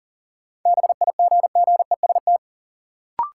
Morse code, heard as a keyed tone, about 40 wpm, 700 Hz. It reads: BIGGEST
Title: Sets of 4 Words - New Word Drill - 40wpm